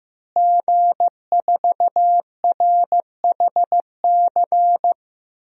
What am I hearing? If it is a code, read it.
Morse code, heard as a keyed tone, about 16 wpm, 700 Hz. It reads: G4RHC
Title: DXCC-100 Call Signs - 5 Characters - Repeated 3x - 15wpm